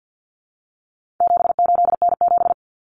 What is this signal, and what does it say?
Morse code, heard as a keyed tone, about 50 wpm, 700 Hz. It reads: 78N7
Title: Common Cut Numbers - 4-Digit 50wpm